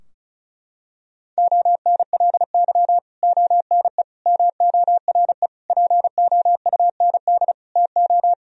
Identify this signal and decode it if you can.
Morse code, heard as a keyed tone, about 36 wpm, 700 Hz. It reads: ONLY ONE MORE POUND TO
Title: Sentences from Top 1000 Words - Review Part 2 - 35wpm